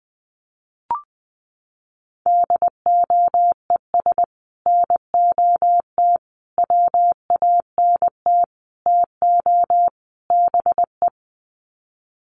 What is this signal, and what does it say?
Morse code, heard as a keyed tone, about 20 wpm, 700 Hz. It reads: DOES NOT WANT TO BE
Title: Sets of 5 Words Encore 20wpm